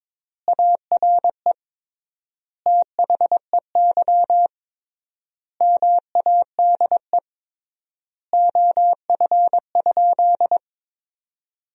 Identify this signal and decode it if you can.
Morse code, heard as a keyed tone, about 22 wpm, 700 Hz. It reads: ARE THEY MADE OF?
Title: 3x Word-Spacing - Sentences from Top 200 Words - 22wpm